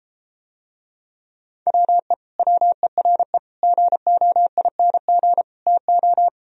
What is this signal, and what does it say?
Morse code, heard as a keyed tone, about 33 wpm, 700 Hz. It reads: WE WERE GOING TO